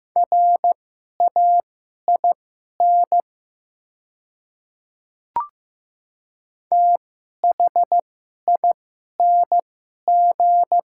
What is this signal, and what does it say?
Morse code, heard as a keyed tone, about 15 wpm, spaced as for 11 wpm, 700 Hz. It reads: RAIN THING
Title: 2x Character Spacing - Top 300 Words - 15wpm